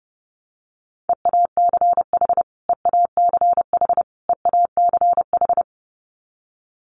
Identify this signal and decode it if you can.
Morse code, heard as a keyed tone, about 30 wpm, 700 Hz. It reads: EACH EACH EACH
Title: Top 200 Words Repeated 30wpm